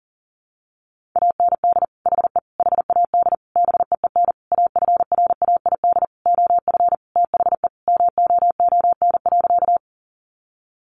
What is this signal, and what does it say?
Morse code, heard as a keyed tone, about 40 wpm, 700 Hz. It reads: AND HE HAD BEEN AFRAID OF THE MOON.
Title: Sentences from Top 1500 Words 40wpm